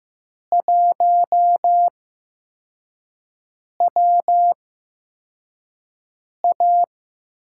Morse code, heard as a keyed tone, about 15 wpm, spaced as for 4 wpm, 700 Hz. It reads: 1WA
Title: USA Call Signs - 4 Characters - 8x Character Spacing - 15wpm